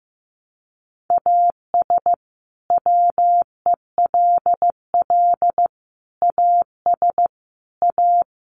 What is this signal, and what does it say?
Morse code, heard as a keyed tone, about 15 wpm, 700 Hz. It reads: AS WELL AS A
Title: Sets of 4 Words 15wpm